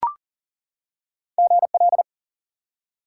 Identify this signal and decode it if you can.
Morse code, heard as a keyed tone, about 40 wpm, 700 Hz. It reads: GL
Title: Top 90 QSO Elements 40wpm